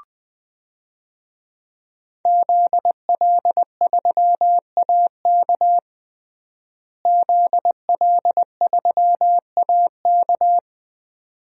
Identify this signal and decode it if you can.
Morse code, heard as a keyed tone, about 20 wpm, 700 Hz. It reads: ZL3AK ZL3AK
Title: DXCC-100 Call Signs - 5 Characters - Repeated 2x - 20wpm